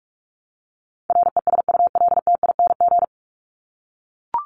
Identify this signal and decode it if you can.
Morse code, heard as a keyed tone, about 44 wpm, 700 Hz. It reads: RESULTING
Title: ING Suffix - Top 100 Words with Base Words 45wpm